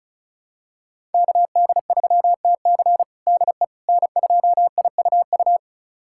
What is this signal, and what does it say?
Morse code, heard as a keyed tone, about 34 wpm, 700 Hz. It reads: KD3TC DE N2IUU
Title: USA Callsign DE Callsign - 5 Characters - 35wpm